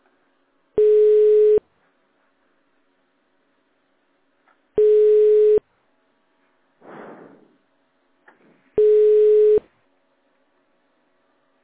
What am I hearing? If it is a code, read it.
Telephone call progress tone (425 Hz, 1 s on, 3 s off), heard as a telephone network's tone, about 420 Hz. Ringback tone.